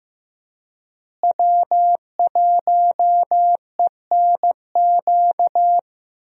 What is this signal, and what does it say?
Morse code, heard as a keyed tone, about 16 wpm, 700 Hz. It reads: W1ENQ
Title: USA Call Signs - 5 Characters - Repeated 3x - 15wpm